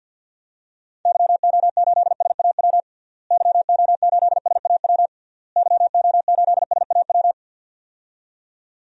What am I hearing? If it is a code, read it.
Morse code, heard as a keyed tone, about 53 wpm, 700 Hz. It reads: YO8SAW YO8SAW YO8SAW
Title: DXCC-100 Call Signs - 6 Characters - Repeated 3x - 50wpm